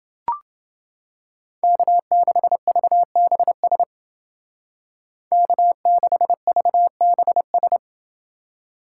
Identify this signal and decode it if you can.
Morse code, heard as a keyed tone, about 29 wpm, 700 Hz. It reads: K6VBS K6VBS